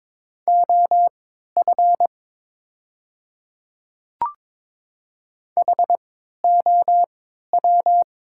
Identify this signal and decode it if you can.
Morse code, heard as a keyed tone, about 22 wpm, spaced as for 12 wpm, 700 Hz. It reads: OF HOW
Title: 3x Character Spacing - Top 100 Words - 22wpm